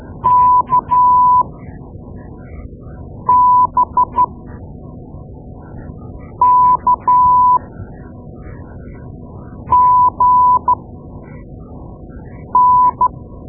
Morse code, heard as a keyed tone, about 12 wpm, spaced as for 4 wpm, 1000 Hz. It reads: KBKGN
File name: Morse Lessons / Morse Lessons / Tape-2A